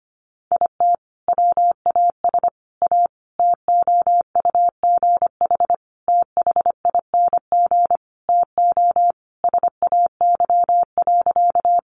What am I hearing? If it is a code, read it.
Morse code, heard as a keyed tone, about 25 wpm, 700 Hz. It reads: IT WAS A TOUGH THING TO SAY.
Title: Sentences from Top 1200 Words 25wpm